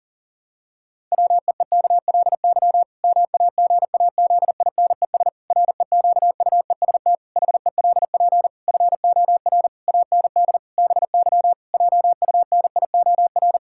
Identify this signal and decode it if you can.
Morse code, heard as a keyed tone, about 40 wpm, 700 Hz. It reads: WEEKLY MAGAZINES REQUEST HELP FOR AND BY JUNIOR
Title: Pangram Challenge v2 - 40wpm